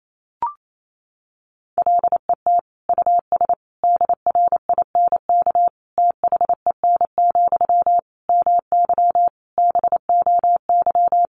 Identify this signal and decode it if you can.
Morse code, heard as a keyed tone, about 28 wpm, 700 Hz. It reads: LET US DRINK THEN, MY BOY
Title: Sentences from Top 1500 Words 28wpm